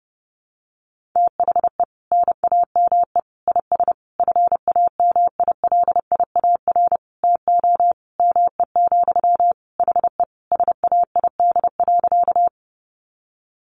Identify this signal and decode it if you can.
Morse code, heard as a keyed tone, about 30 wpm, 700 Hz. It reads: THE NAME IS FAMILIAR TO ME, HE SAID.
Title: Sentences from Top 1000 Words 30wpm